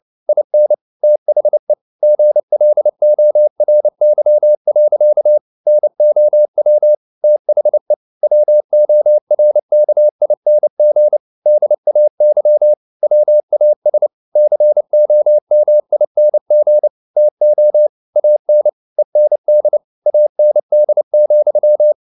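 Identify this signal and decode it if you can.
Morse code, heard as a keyed tone, about 29 wpm, 600 Hz. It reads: IN THE GLORY. NOW THE WORKING DAY WAS COMING TO AN END AND,
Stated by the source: Morse code at 29 wpm, sending Ken Follett's novel Never in English, from CW med Gnister.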